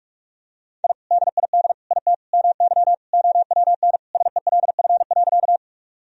Morse code, heard as a keyed tone, about 41 wpm, 700 Hz. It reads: I DID IT MY OWN SELF.
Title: Sentences from Top 1000 Words - Review Part 1 - 45wpm